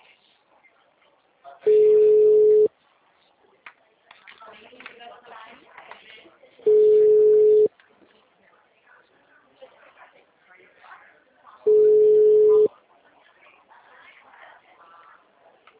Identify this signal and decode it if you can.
Ringback tone (Europe, 425 Hz), with speech.
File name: out- -2019- - - .4735